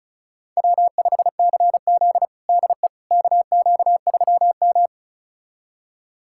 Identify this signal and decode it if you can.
Morse code, heard as a keyed tone, about 36 wpm, 700 Hz. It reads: W5CZ DE KQ3M
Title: USA Callsign DE Callsign - 4 Characters - 35wpm